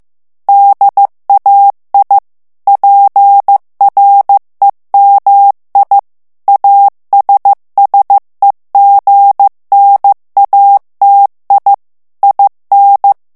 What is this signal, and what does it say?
Morse code, heard as a keyed tone, about 15 wpm, 800 Hz. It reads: DAI PREMI ASSEGNATI IN
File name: morse mp3 / ITALIEN texte clair 2 vitesse 14 mots minute